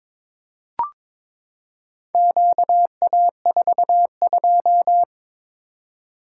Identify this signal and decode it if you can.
Morse code, heard as a keyed tone, about 22 wpm, 700 Hz. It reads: QA42
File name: practice / 22wpm / Four Letter-Number Combinations - 22wpm